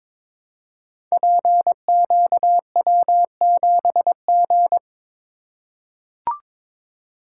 Morse code, heard as a keyed tone, about 22 wpm, 700 Hz. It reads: PQW7G
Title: Code Groups - 5 Letter:Numbers - 22wpm